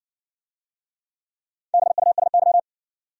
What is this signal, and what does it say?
Morse code, heard as a keyed tone, about 60 wpm, 700 Hz. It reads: BUSY